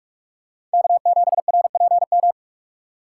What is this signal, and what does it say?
Morse code, heard as a keyed tone, about 46 wpm, 700 Hz. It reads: K7RPM
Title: USA Call Signs - 5 Characters - 4x Character Spacing - 45wpm